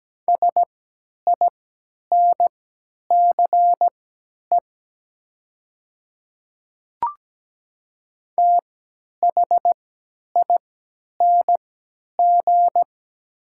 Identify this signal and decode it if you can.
Morse code, heard as a keyed tone, about 17 wpm, spaced as for 10 wpm, 700 Hz. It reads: SINCE THING
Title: 3x Character Spacing - Top 300 Words - 17wpm